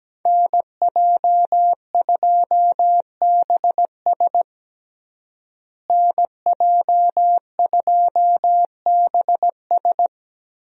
Morse code, heard as a keyed tone, about 17 wpm, 700 Hz. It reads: NJ2BS NJ2BS
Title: USA Call Signs - 5 Characters - Repeated 3x - 17wpm